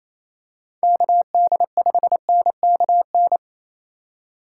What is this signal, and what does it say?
Morse code, heard as a keyed tone, about 28 wpm, 700 Hz. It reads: KD5NKN
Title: USA Call Signs - 6 Characters - 3x Character Spacing - 28wpm